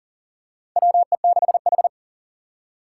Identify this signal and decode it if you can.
Morse code, heard as a keyed tone, about 40 wpm, 700 Hz. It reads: WE6H
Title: USA Call Signs - 4 Characters - 3x Character Spacing - 40wpm